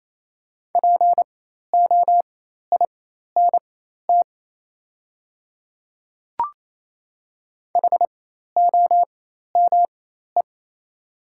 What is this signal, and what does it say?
Morse code, heard as a keyed tone, about 27 wpm, spaced as for 13 wpm, 690 Hz. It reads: POINT HOME
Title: 4x Character Spacing - Top 200 Words - 28wpm